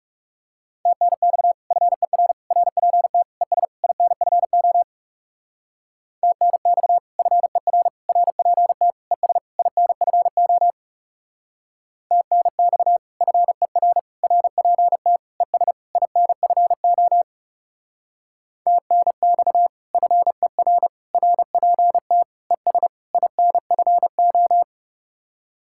Morse code, heard as a keyed tone, about 37 wpm, 700 Hz. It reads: TNX FER RPT ES INFO TNX FER RPT ES INFO TNX FER RPT ES INFO TNX FER RPT ES INFO
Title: Speed-Racing Common QSO Phrases Encore - 30wpm